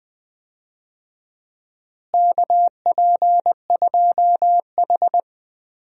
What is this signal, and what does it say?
Morse code, heard as a keyed tone, about 20 wpm, 700 Hz. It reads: KP2H
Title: DXCC-100 Call Signs - 4 Characters - Repeated 2x - 20wpm